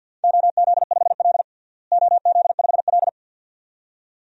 Morse code, heard as a keyed tone, about 53 wpm, 700 Hz. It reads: OZ5L OZ5L
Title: DXCC-100 Call Signs - 4 Characters - Repeated 2x - 50wpm